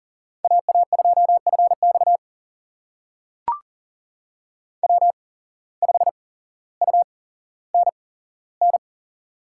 Morse code, heard as a keyed tone, about 40 wpm, spaced as for 17 wpm, 700 Hz. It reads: AA2FX W5UNN
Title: USA Call Signs - 5 Characters - 8x Character Spacing - 40wpm